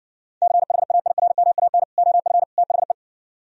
Morse code, heard as a keyed tone, about 59 wpm, 700 Hz. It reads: CHAIRMAN OF THE